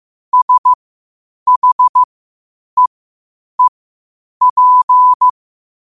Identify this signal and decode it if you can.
Morse code, heard as a keyed tone, about 15 wpm, spaced as for 8 wpm, 1000 Hz. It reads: SHEEP
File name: qotdfiles / 8wpm